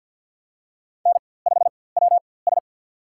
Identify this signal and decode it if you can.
Morse code, heard as a keyed tone, about 52 wpm, spaced as for 23 wpm, 700 Hz. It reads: N5WS
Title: USA Call Signs - 4x Character Spacing - 50wpm